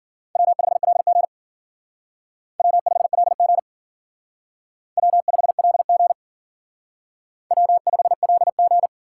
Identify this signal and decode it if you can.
Morse code, heard as a keyed tone, about 51 wpm, 700 Hz. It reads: W5LG W5LG W5LG W5LG